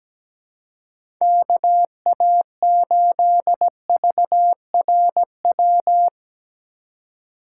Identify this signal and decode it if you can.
Morse code, heard as a keyed tone, about 17 wpm, 700 Hz. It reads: KA8VRW
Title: USA Call Signs - 6 Characters - 2x Character Spacing - 17wpm